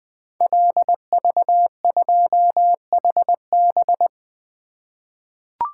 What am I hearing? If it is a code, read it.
Morse code, heard as a keyed tone, about 20 wpm, 700 Hz. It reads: LV2HB
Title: Code Groups - 5 Letter:Numbers - 20wpm